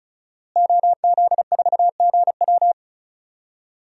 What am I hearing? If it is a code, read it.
Morse code, heard as a keyed tone, about 34 wpm, 700 Hz. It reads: OZ4GW
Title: DXCC-100 Call Signs - 5 Characters - Repeated 2x - 35wpm